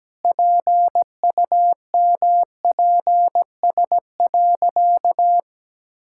Morse code, heard as a keyed tone, about 17 wpm, 700 Hz. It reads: PUMPS.
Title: Pangram Challenge v2 - 17wpm